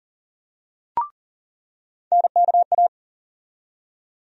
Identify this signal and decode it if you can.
Morse code, heard as a keyed tone, about 40 wpm, 700 Hz. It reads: NKA
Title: USA Call Sign Postfixes - 3 Letters 40wpm